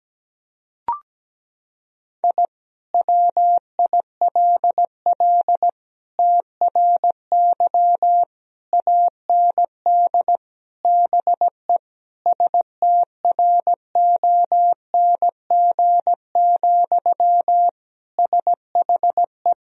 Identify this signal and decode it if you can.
Morse code, heard as a keyed tone, about 17 wpm, 700 Hz. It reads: I WILL TRY AND BE STRONG, SHE
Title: Sentences from Top 1000 Words - Review Part 3 - 17wpm